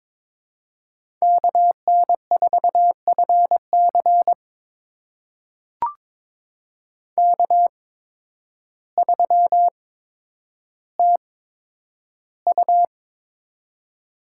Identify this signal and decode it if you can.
Morse code, heard as a keyed tone, about 22 wpm, spaced as for 10 wpm, 700 Hz. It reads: KN4FC K3TU
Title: USA Call Signs - 8x Character Spacing - 22wpm